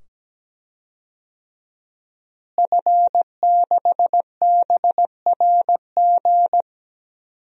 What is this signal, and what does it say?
Morse code, heard as a keyed tone, about 17 wpm, 700 Hz. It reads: F6BRG
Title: DXCC-100 Call Signs - 5 Characters - Repeated 2x - 17wpm